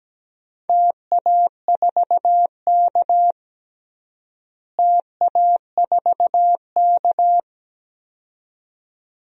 Morse code, heard as a keyed tone, about 17 wpm, 700 Hz. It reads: TA4K TA4K
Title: DXCC-100 Call Signs - 4 Characters - Repeated 2x - 17wpm